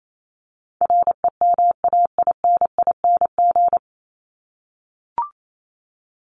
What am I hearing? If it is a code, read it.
Morse code, heard as a keyed tone, about 28 wpm, 700 Hz. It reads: REMAINING
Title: ING Suffix - Top 100 Words with Base Words 28wpm